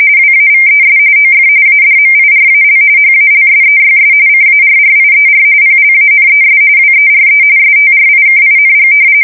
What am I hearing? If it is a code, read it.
RTTY, about 45.45 baud. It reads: WAVES OF ADMIRAL HALSEYS CARRIER PLANES WERE STRIKING
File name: fsk170 / battleshipnj